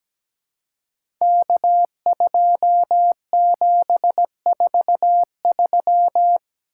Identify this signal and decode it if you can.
Morse code, heard as a keyed tone, about 17 wpm, 700 Hz. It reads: K2743
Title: Code Groups - 5 Letter:Numbers - 17wpm